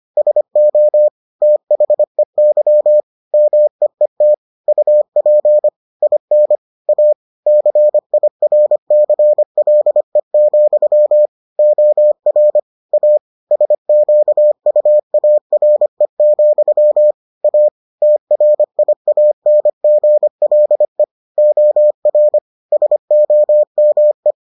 Morse code, heard as a keyed tone, about 25 wpm, 600 Hz. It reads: SO THEY MEET UP IN A CIRCLE, OR A SQUARE, A TRIANGLE OR SOME